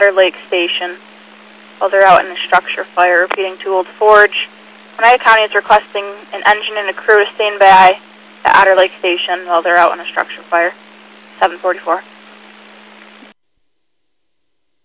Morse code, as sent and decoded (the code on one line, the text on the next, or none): none